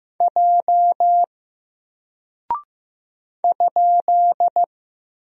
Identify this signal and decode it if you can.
Morse code, heard as a keyed tone, about 15 wpm, 700 Hz. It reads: J ?
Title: Single Character (Letter:Number:Punctuation:Prosign) 15wpm